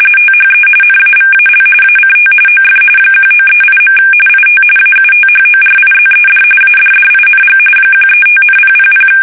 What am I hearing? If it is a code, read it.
RTTY, about 45.45 baud. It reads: THE NAVAL ACT OF 1938. IT AUTHORIZED ADDING 135,000
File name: fsk850 / battleshipnj